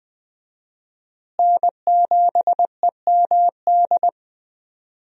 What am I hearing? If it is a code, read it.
Morse code, heard as a keyed tone, about 20 wpm, 700 Hz. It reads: N7EMD
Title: USA Call Signs - 5 Characters - Repeated 3x - 20wpm